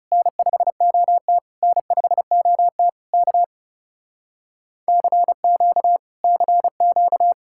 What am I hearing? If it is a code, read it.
Morse code, heard as a keyed tone, about 40 wpm, 700 Hz. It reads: N5OT N5OT K CQ CQ